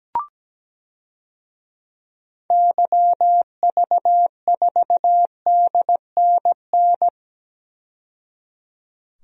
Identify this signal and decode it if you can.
Morse code, heard as a keyed tone, about 17 wpm, 700 Hz. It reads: YV4DNN
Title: DXCC-100 Call Signs - 6 Characters - 17wpm